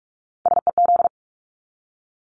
Morse code, heard as a keyed tone, about 44 wpm, 700 Hz. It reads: SEZ
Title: Abbreviations Repeated 3x - 45wpm